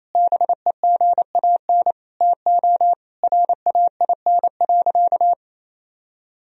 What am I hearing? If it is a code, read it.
Morse code, heard as a keyed tone, about 28 wpm, 700 Hz. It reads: BEGAN TO RAIN.
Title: Sentences from Top 1000 Words - Review Part 1 - 28wpm